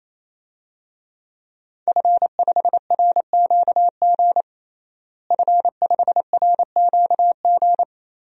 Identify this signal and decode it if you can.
Morse code, heard as a keyed tone, about 28 wpm, 700 Hz. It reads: F5RQG F5RQG
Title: DXCC-100 Call Signs - 5 Characters - Repeated 3x - 28wpm